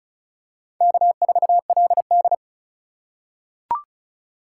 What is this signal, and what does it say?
Morse code, heard as a keyed tone, about 36 wpm, 700 Hz. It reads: K4LD
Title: USA Call Signs - 4 Characters - 4x Character Spacing - 35wpm